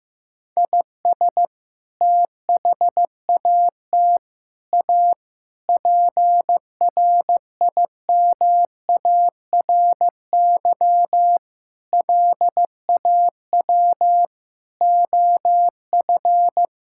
Morse code, heard as a keyed tone, about 15 wpm, 700 Hz. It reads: IS THAT A PRIMARY LAW OF
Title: Sentences from Top 1000 Words - Review Part 5 - 15wpm